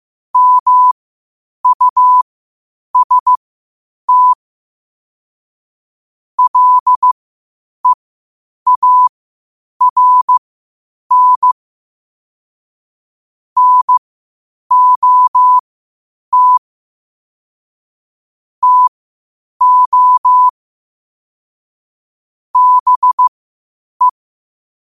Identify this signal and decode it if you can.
Morse code, heard as a keyed tone, about 15 wpm, spaced as for 8 wpm, 1000 Hz. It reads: MUST LEARN NOT TO BE
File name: qotdfiles / 8wpm